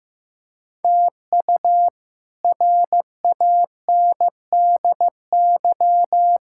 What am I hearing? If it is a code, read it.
Morse code, heard as a keyed tone, about 15 wpm, 700 Hz. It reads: TU RANDY
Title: K1USN SST QSO Practice - 15wpm